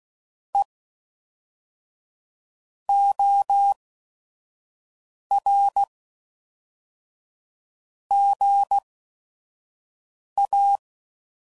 Morse code, heard as a keyed tone, about 16 wpm, spaced as for 5 wpm, 800 Hz. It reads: E OR GA
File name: CW Code Groups @5WPM